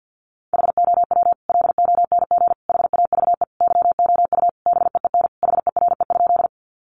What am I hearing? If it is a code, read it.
Morse code, heard as a keyed tone, about 50 wpm, 700 Hz. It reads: HOW LONG HAVE YOU BEEN HERE?
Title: Sentences from Top 200 Words 50wpm